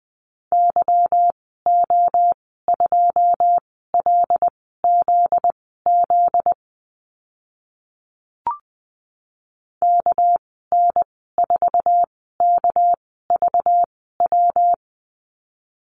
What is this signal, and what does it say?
Morse code, heard as a keyed tone, about 20 wpm, spaced as for 15 wpm, 700 Hz. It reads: YO2LZZ KN4KVW
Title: 245 Lesson 39g - Letter z - Call Signs 20wpm